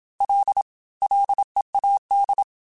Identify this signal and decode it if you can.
Morse code, heard as a keyed tone, about 27 wpm, 800 Hz. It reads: L LEAD
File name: F and L / F and L mp3 / F and L Prefixes 3x2 Twice @25